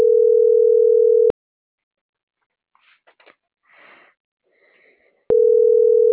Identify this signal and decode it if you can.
Ringback tone (North America), 440 + 480 Hz.